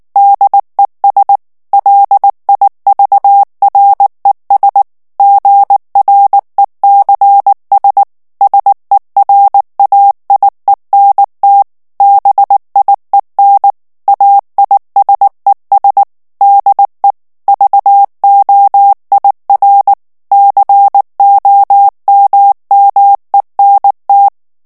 Morse code, heard as a keyed tone, about 19 wpm, 800 Hz. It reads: DES LIVRES GRECS SERAIENT BIEN AISES DE VOIR COMMENT
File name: CW / FICHIERS-MP3 / FRANCAIS TEXTE CLAIR 2 VITESSE 18 MOTS MINUTES